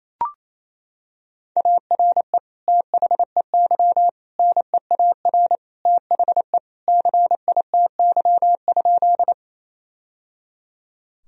Morse code, heard as a keyed tone, about 27 wpm, 700 Hz. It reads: ARE THEY NEAR THE CITY?
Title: Sentences from Top 1000 Words - Review Part 1 - 28wpm